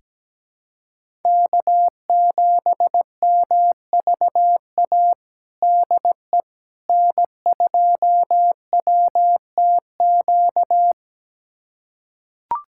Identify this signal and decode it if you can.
Morse code, heard as a keyed tone, about 17 wpm, 700 Hz. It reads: K7MVA DE N2WTQ
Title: USA Callsign DE Callsign - 5 Characters - 17wpm